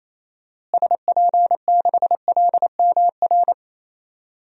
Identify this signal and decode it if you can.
Morse code, heard as a keyed tone, about 28 wpm, 700 Hz. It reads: SP6LMR
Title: DXCC-100 Call Signs - 6 Characters - Repeated 2x - 28wpm